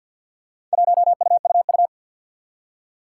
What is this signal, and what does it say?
Morse code, heard as a keyed tone, about 54 wpm, 700 Hz. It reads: 1UUU
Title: Four Challenging Letter-Number Combinations 50wpm